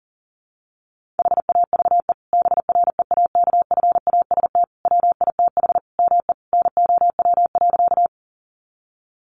Morse code, heard as a keyed tone, about 40 wpm, 700 Hz. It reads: HAVE BREAKFAST WITH ME NOW.
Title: Sentences from Top 1500 Words 40wpm